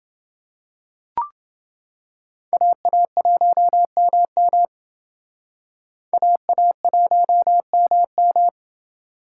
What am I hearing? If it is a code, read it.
Morse code, heard as a keyed tone, about 28 wpm, 700 Hz. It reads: AA1MM AA1MM